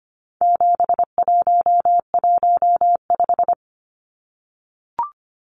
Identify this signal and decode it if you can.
Morse code, heard as a keyed tone, about 25 wpm, 700 Hz. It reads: 7115
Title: Numbers - 4-Digit 25wpm